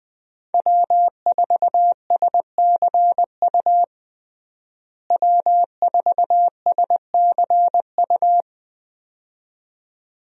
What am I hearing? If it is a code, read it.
Morse code, heard as a keyed tone, about 21 wpm, 700 Hz. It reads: W4SCU W4SCU
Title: USA Call Signs - 5 Characters - Repeated 2x - 20wpm